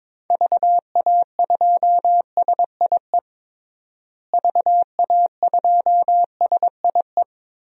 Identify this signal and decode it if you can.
Morse code, heard as a keyed tone, about 22 wpm, 700 Hz. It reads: VA2SIE VA2SIE
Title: DXCC-100 Call Signs - 6 Characters - Repeated 3x - 22wpm